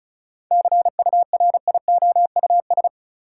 Morse code, heard as a keyed tone, about 34 wpm, 700 Hz. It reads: CURIOUS
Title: Seven Letter Words - 35wpm